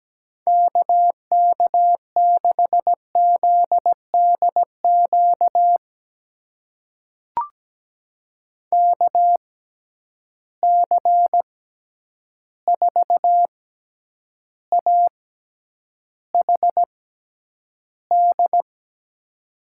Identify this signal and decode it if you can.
Morse code, heard as a keyed tone, about 17 wpm, spaced as for 9 wpm, 700 Hz. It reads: KK6ZDQ KC4AHD
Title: USA Call Signs - 6 Characters - 6x Character Spacing - 17wpm